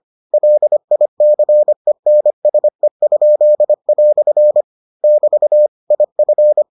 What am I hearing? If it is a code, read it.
Morse code, heard as a keyed tone, about 25 wpm, 600 Hz. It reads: LICENSE?" <BT> IF